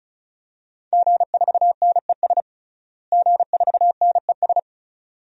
Morse code, heard as a keyed tone, about 36 wpm, 700 Hz. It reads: G4NES G4NES
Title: DXCC-100 Call Signs - 5 Characters - Repeated 2x - 35wpm